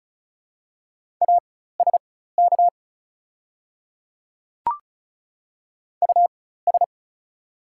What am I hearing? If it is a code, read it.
Morse code, heard as a keyed tone, about 34 wpm, spaced as for 16 wpm, 700 Hz. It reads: ASK US
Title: 4x Character Spacing - Top 200 Words - 35wpm